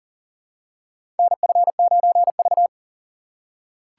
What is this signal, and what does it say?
Morse code, heard as a keyed tone, about 40 wpm, 700 Hz. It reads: NF9V
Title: USA Call Signs - 4 Characters - 4x Character Spacing - 40wpm